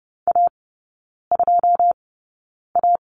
Morse code, heard as a keyed tone, about 30 wpm, spaced as for 9 wpm, 700 Hz. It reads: A2A
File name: course / 30:10wpm / 087 Lesson 15e - Number 5 - Three Characters 30wpm